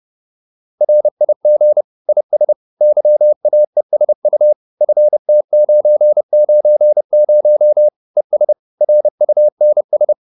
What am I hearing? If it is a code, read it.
Morse code, heard as a keyed tone, about 30 wpm, 600 Hz. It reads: RIG IS YAESU FT990 ES RUNS